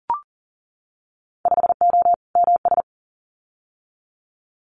Morse code, heard as a keyed tone, about 40 wpm, 700 Hz. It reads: 5O MS
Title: Winter Field Day Exchanges Encore 40wpm